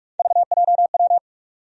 Morse code, heard as a keyed tone, about 46 wpm, 700 Hz. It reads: VJW
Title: USA Call Sign Postfixes - 3 Letters 45wpm